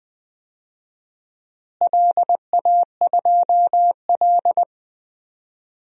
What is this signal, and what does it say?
Morse code, heard as a keyed tone, about 20 wpm, 700 Hz. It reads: LA2L
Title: DXCC-100 Call Signs - 4 Characters - Repeated 2x - 20wpm